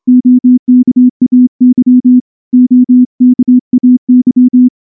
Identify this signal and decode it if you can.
Morse code, heard as a keyed tone, about 25 wpm, 250 Hz. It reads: OKAY OKAY
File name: morse / morse-training-25WPM